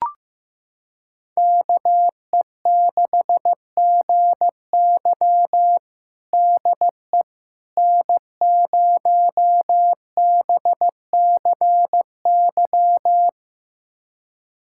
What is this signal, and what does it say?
Morse code, heard as a keyed tone, about 15 wpm, 700 Hz. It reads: KE6GY DE N0BCY
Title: USA Callsign DE Callsign 15wpm